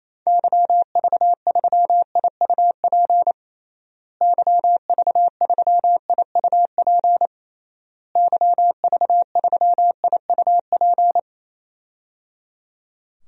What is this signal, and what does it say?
Morse code, heard as a keyed tone, about 28 wpm, 700 Hz. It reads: YV3IUP YV3IUP YV3IUP
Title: DXCC-100 Call Signs - 6 Characters - Repeated 3x - 28wpm